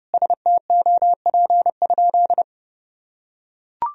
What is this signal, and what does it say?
Morse code, heard as a keyed tone, about 30 wpm, 700 Hz. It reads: STOP?
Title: Sentences from Top 1000 Words - Review Part 1 - 30wpm